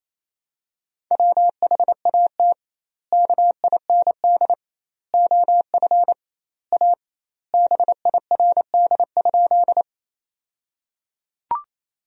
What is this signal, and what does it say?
Morse code, heard as a keyed tone, about 28 wpm, 700 Hz. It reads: WHAT KIND OF A BIRD?
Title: 2x Word-Spacing - Sentences from Top 400 Words - 28wpm